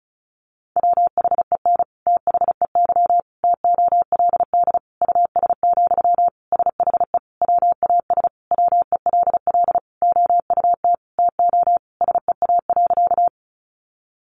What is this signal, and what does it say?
Morse code, heard as a keyed tone, about 34 wpm, 700 Hz. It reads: WHEN THEY TOLD US, SHE WAS WELL OUT TO SEA.
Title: Sentences from Top 400 Words 35wpm